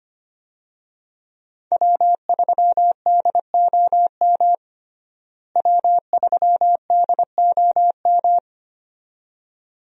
Morse code, heard as a keyed tone, about 25 wpm, 700 Hz. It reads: W3DOM W3DOM
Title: USA Call Signs - 5 Characters - Repeated 2x - 25wpm